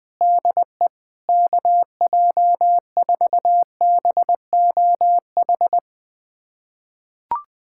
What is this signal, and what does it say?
Morse code, heard as a keyed tone, about 20 wpm, 700 Hz. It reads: DE KJ4BOH
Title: USA Callsign DE Callsign - 6 Characters - 20wpm